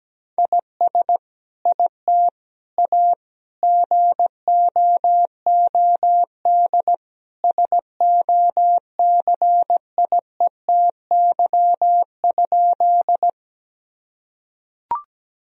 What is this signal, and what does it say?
Morse code, heard as a keyed tone, about 17 wpm, 700 Hz. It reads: IS IT A GOOD SOCIETY?
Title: Sentences from Top 1000 Words - Review Part 4 - 17wpm